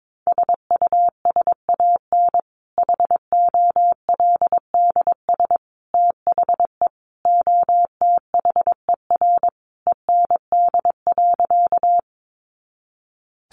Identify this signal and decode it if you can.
Morse code, heard as a keyed tone, about 22 wpm, 700 Hz. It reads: SUSAN HOLDS THE OTHER END.
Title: Easy Phrases 22wpm